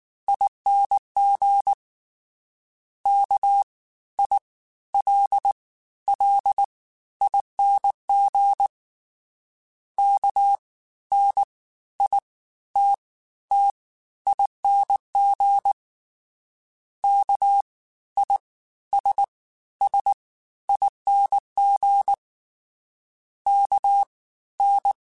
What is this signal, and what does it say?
Morse code, heard as a keyed tone, about 19 wpm, spaced as for 13 wpm, 800 Hz. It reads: ING KILLING KNITTING KISSING KN
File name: Special INGs-mp3s / Special INGs-mp3s / Fast ing @18